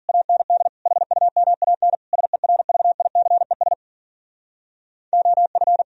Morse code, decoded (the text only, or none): AND HUMAN SERVICES OF